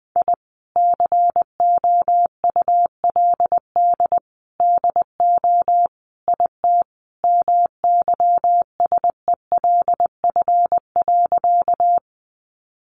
Morse code, heard as a keyed tone, about 20 wpm, 700 Hz. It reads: I COULD DO IT MYSELF.
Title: Sentences from Top 1300 Words 20wpm